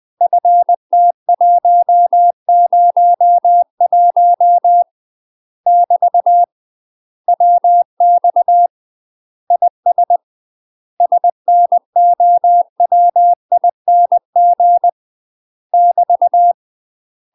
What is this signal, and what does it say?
Morse code, heard as a keyed tone, about 20 wpm, 700 Hz. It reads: FT101 <BT> WX IS SNOWING <BT>
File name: UK MASTER QSOs TRAINING / 20WPM / UK MASTER QSOs 0063